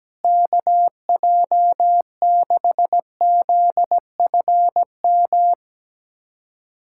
Morse code, heard as a keyed tone, about 17 wpm, 700 Hz. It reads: KJ6ZFM
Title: USA Callsign DE Callsign 17wpm